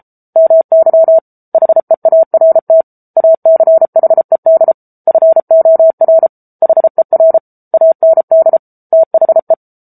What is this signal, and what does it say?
Morse code, heard as a keyed tone, about 34 wpm, 650 Hz. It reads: MY HEART ACHED FOR HER AND THE
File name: Book Two / 07-Chapter 7 - 33 WPM